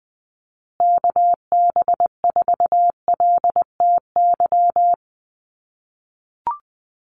Morse code, heard as a keyed tone, about 20 wpm, 700 Hz. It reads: KB4LTY
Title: USA Call Signs Repeated 2x 20wpm